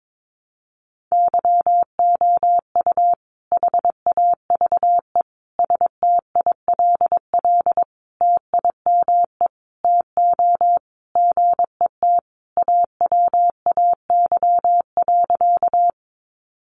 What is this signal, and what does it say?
Morse code, heard as a keyed tone, about 22 wpm, 700 Hz. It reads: YOU HAVE STILL TIME TO GET AWAY.
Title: Sentences from Top 600 Words 22wpm